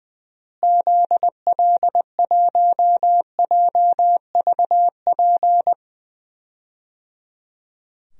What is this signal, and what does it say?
Morse code, heard as a keyed tone, about 20 wpm, 700 Hz. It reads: ZL1JVP
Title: DXCC-100 Call Signs - 6 Characters - Repeated 3x - 20wpm